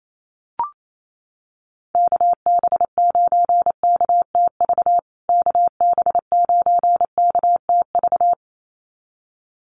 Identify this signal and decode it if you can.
Morse code, heard as a keyed tone, about 28 wpm, 700 Hz. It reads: KB9KTV KB9KTV
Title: USA Call Signs Repeated 2x 28wpm